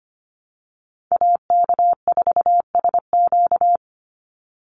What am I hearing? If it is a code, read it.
Morse code, heard as a keyed tone, about 25 wpm, 700 Hz. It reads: AK4SQ
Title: Call Signs Repeated 2x - 25wpm